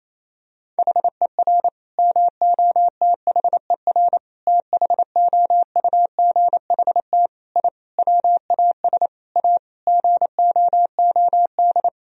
Morse code, heard as a keyed tone, about 28 wpm, 700 Hz. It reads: HER MOTHER THOUGHT I WAS A GOOD
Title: Sentences from Top 1000 Words - Review Part 4 - 28wpm